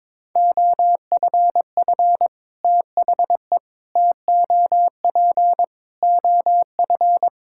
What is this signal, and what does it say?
Morse code, heard as a keyed tone, about 22 wpm, 700 Hz. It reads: OFF THE TOP OF
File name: practice / 22wpm / Sets of 6 Words - 22wpm